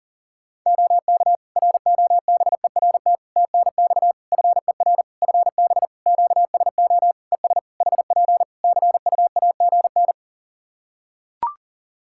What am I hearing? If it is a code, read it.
Morse code, heard as a keyed tone, about 40 wpm, 700 Hz. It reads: OK ROBERT TNX FER FB QSO ES HP CUAGN
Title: Practice QSO - Exchange 4 - Ending - 40wpm